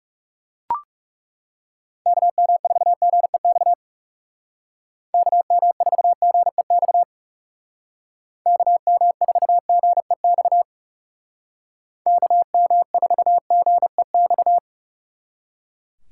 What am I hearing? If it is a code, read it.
Morse code, heard as a keyed tone, about 36 wpm, 700 Hz. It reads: KM4GEX KM4GEX KM4GEX KM4GEX
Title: Speed Racing - USA 2x3 Call Signs 30wpm